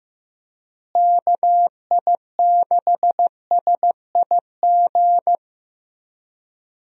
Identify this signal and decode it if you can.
Morse code, heard as a keyed tone, about 15 wpm, 700 Hz. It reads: KI6SIG
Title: USA Call Signs - 6 Characters - Repeated 3x - 15wpm